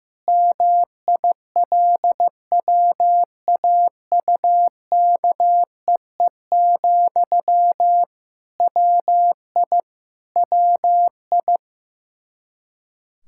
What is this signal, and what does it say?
Morse code, heard as a keyed tone, about 15 wpm, 700 Hz. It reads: MILWAUKEE, WI WI
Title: Practice QSO - Exchange 1 - Beginning - 15wpm